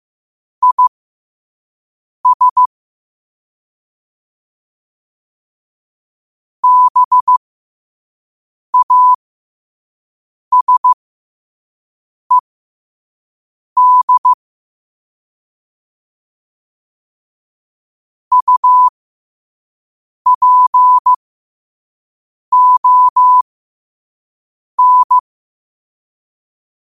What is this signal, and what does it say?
Morse code, heard as a keyed tone, about 15 wpm, spaced as for 5 wpm, 1000 Hz. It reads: IS BASED UPON
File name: qotdfiles / 5wpm